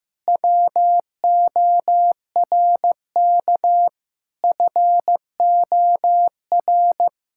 Morse code, heard as a keyed tone, about 15 wpm, 700 Hz. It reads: WORK FOR
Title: Sentences from Top 1000 Words - Review Part 6 - 15wpm